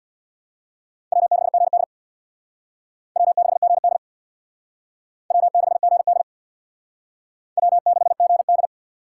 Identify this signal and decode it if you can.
Morse code, heard as a keyed tone, about 64 wpm, 700 Hz. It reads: W6GD W6GD W6GD W6GD